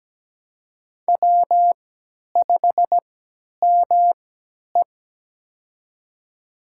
Morse code, heard as a keyed tone, about 17 wpm, spaced as for 10 wpm, 700 Hz. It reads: W5ME
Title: USA Call Signs - 4 Characters - 3x Character Spacing - 17wpm